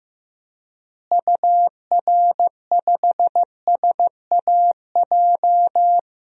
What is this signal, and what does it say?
Morse code, heard as a keyed tone, about 15 wpm, 700 Hz. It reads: UR5SAJ
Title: DXCC-100 Call Signs - 6 Characters - Repeated 3x - 15wpm